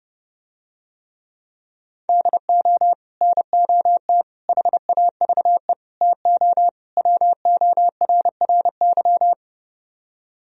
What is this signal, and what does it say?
Morse code, heard as a keyed tone, about 29 wpm, 700 Hz. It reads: DO NOT HAVE TO WORRY